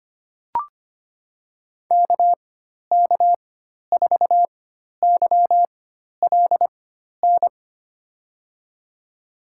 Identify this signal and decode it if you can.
Morse code, heard as a keyed tone, about 25 wpm, spaced as for 12 wpm, 700 Hz. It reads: KK4YLN
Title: USA Call Signs - 6 Characters - 4x Character Spacing - 25wpm